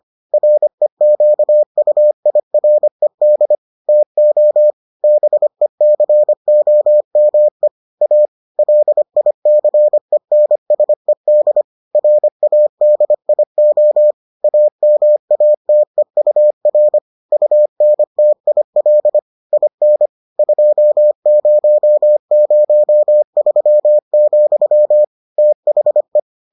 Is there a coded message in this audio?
Yes: Morse code from 0.3 to 26.2 s, heard as a keyed tone, about 25 wpm, 600 Hz. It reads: REQUIRED TO BECOME A LICENSED RADIO AMATEUR UNTIL IN 2003, THE